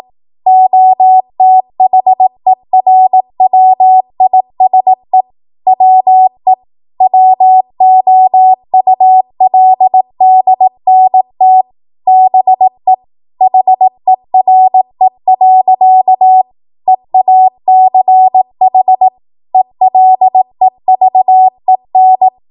Morse code, heard as a keyed tone, about 18 wpm, 750 Hz. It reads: OTHERWISE WE WOULDNT BE HERE. EACH ELEVEN
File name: Morse / 18WPM